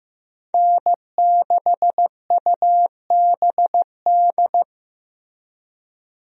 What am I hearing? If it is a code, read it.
Morse code, heard as a keyed tone, about 15 wpm, 700 Hz. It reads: N6UBD
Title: USA Call Signs - 5 Characters - Repeated 3x - 15wpm